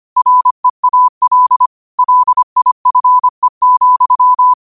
Morse code, heard as a keyed tone, about 25 wpm, 1000 Hz. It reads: REAL LIFE,
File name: qotdfiles / 25wpm